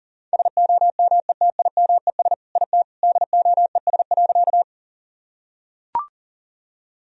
Morse code, heard as a keyed tone, about 40 wpm, 700 Hz. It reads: SOMETIMES IT DOES.
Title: Sentences from Top 1000 Words - Review Part 3 - 40wpm